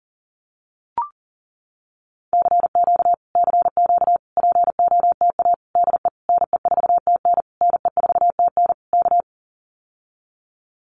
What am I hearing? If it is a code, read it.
Morse code, heard as a keyed tone, about 40 wpm, 700 Hz. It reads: CQ CQ POTA DE NE4TN NE4TN K
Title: Common QSO Phrases Encore 40wpm